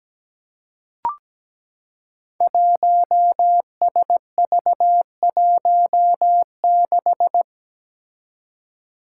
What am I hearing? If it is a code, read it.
Morse code, heard as a keyed tone, about 17 wpm, 700 Hz. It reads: 1SV16
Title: Five Challenging Letter-Number Combinations 17wpm